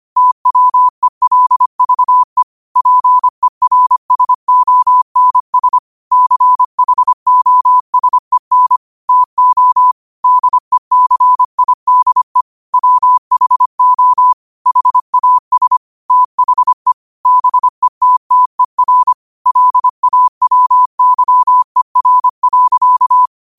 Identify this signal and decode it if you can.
Morse code, heard as a keyed tone, about 25 wpm, 1000 Hz. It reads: TWELVE PERSONS CHOSEN TO DECIDE WHO HAS THE BETTER LAWYER.
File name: qotdfiles / 25wpm